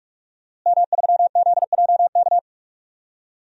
Morse code, heard as a keyed tone, about 46 wpm, 700 Hz. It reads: M3ZJK
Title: DXCC-100 Call Signs - 5 Characters - Repeated 2x - 45wpm